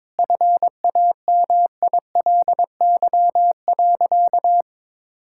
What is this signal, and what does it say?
Morse code, heard as a keyed tone, about 22 wpm, 700 Hz. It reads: FAMILY.
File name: Difficult Sentences with Hints - 22wpm